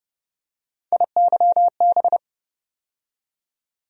Morse code, heard as a keyed tone, about 30 wpm, 700 Hz. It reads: IYB
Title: USA Call Sign Postfixes - 3 Letters 30wpm